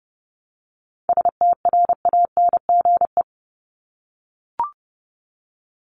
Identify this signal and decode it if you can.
Morse code, heard as a keyed tone, about 30 wpm, 700 Hz. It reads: STRANGE
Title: Top 1501-1750 Words 30wpm